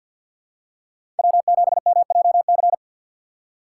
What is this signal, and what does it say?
Morse code, heard as a keyed tone, about 48 wpm, 700 Hz. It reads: W7MJC
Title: USA Call Signs - 5 Characters - 3x Character Spacing - 50wpm